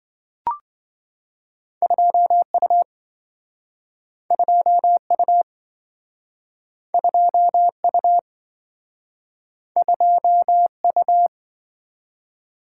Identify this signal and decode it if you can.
Morse code, heard as a keyed tone, about 27 wpm, 700 Hz. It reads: 2U 2U 2U 2U